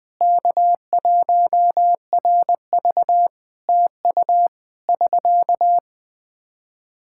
Morse code, heard as a keyed tone, about 20 wpm, 700 Hz. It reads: K1RV TU <SK>
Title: K1USN SST QSO Practice 20wpm